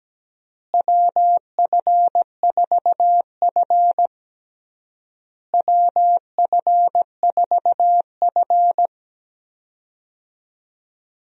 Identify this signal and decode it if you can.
Morse code, heard as a keyed tone, about 17 wpm, 700 Hz. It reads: WF4F WF4F
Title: USA Call Signs - 4 Characters - Repeated 3x - 17wpm